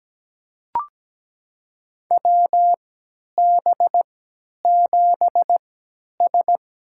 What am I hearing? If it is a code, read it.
Morse code, heard as a keyed tone, about 17 wpm, spaced as for 10 wpm, 700 Hz. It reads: WB7S
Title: USA Call Signs - 3x Character Spacing - 17wpm